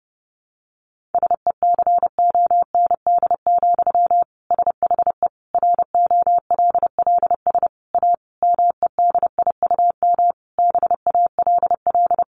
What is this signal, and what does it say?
Morse code, heard as a keyed tone, about 30 wpm, 700 Hz. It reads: SECOND, SHE ROLLS A MEDIUM BALL
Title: Easy Phrases 30wpm